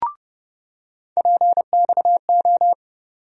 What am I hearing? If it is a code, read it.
Morse code, heard as a keyed tone, about 30 wpm, 700 Hz. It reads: PXO
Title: USA Call Sign Postfixes - 3 Letters 30wpm